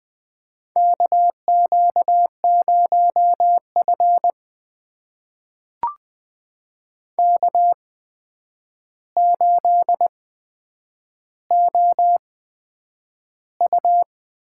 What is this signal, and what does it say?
Morse code, heard as a keyed tone, about 20 wpm, spaced as for 9 wpm, 700 Hz. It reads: KQ0F K8OU
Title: USA Call Signs - 4 Characters - 8x Character Spacing - 20wpm